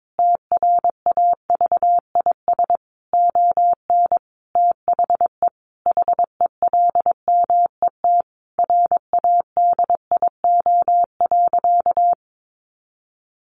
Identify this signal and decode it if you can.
Morse code, heard as a keyed tone, about 22 wpm, 700 Hz. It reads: TRAVIS ON THE HELMET RADIO.
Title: A Sound Of Thunder 22wpm